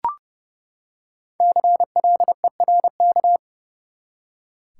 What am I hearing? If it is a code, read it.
Morse code, heard as a keyed tone, about 30 wpm, 700 Hz. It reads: CLERK
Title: Five Letter Words - 30wpm